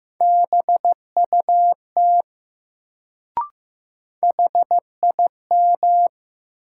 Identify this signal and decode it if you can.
Morse code, heard as a keyed tone, about 15 wpm, 700 Hz. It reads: BUT HIM
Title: Sentences from Top 100 Words - New Word Drill - 15wpm